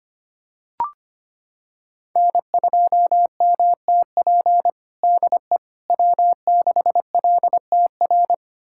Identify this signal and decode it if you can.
Morse code, heard as a keyed tone, about 25 wpm, 700 Hz. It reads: N2MTP DE W6LTR
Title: USA Callsign DE Callsign - 5 Characters - 25wpm